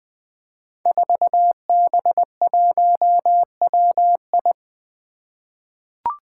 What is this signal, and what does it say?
Morse code, heard as a keyed tone, about 20 wpm, 700 Hz. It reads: 4B1WI
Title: Five Challenging Letter-Number Combinations 20wpm